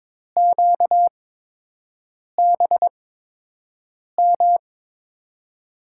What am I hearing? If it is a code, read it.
Morse code, heard as a keyed tone, about 21 wpm, spaced as for 6 wpm, 700 Hz. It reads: QBM
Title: USA Call Signs - 8x Character Spacing - 22wpm